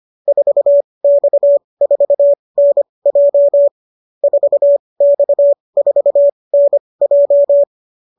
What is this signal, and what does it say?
Morse code, heard as a keyed tone, about 25 wpm, spaced as for 20 wpm, 590 Hz. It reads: 4X4NJ 4X4NJ